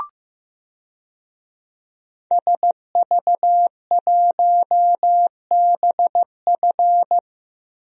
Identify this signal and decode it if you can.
Morse code, heard as a keyed tone, about 15 wpm, 700 Hz. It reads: SV1BF
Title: DXCC-100 Call Signs - 5 Characters - 15wpm